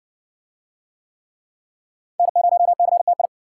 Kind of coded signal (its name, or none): Morse code